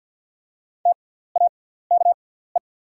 Morse code, heard as a keyed tone, about 48 wpm, spaced as for 17 wpm, 690 Hz. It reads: TAKE
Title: 6x Character Spacing - Top 500 Words - 50wpm